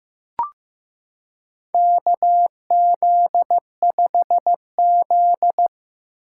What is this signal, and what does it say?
Morse code, heard as a keyed tone, about 15 wpm, 700 Hz. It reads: KZ5Z
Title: K1USN SST QSO Practice - 15wpm